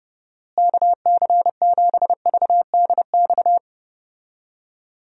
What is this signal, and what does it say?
Morse code, heard as a keyed tone, about 32 wpm, 700 Hz. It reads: KC7VDX
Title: USA Call Signs - 6 Characters - Repeated 3x - 30wpm